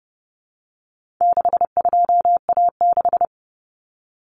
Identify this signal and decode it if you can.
Morse code, heard as a keyed tone, about 30 wpm, 700 Hz. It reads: 62A6
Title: Common Cut Numbers - 4-Digit 30wpm